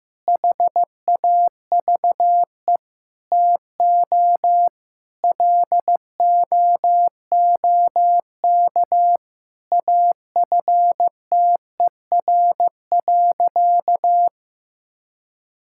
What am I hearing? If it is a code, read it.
Morse code, heard as a keyed tone, about 15 wpm, 700 Hz. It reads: HAVE TO LOOK AFTER.
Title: Sentences from Top 1600 Words 15wpm